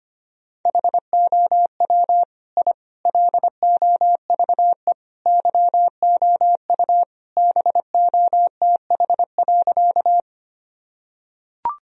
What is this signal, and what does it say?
Morse code, heard as a keyed tone, about 25 wpm, 700 Hz. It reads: HOW I LOVE YOU BOTH.
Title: Sentences from Top 1000 Words - Review Part 1 - 25wpm